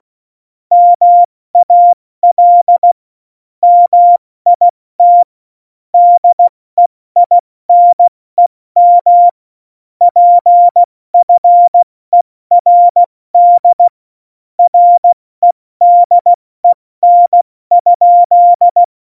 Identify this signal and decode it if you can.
Morse code, heard as a keyed tone, about 16 wpm, 700 Hz. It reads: MAL MIT DEINEM PFERD REDEN?
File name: Morsekurs / Witze / Witz 06. Oktober 2020, 70 F80